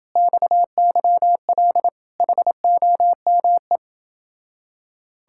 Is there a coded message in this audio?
Yes: Morse code from 0.2 to 3.8 s, heard as a keyed tone, about 27 wpm, 700 Hz. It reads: XYL HOME